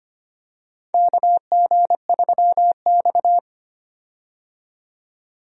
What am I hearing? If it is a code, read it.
Morse code, heard as a keyed tone, about 25 wpm, 700 Hz. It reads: KG3X
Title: USA Call Signs - 4 Characters - 25wpm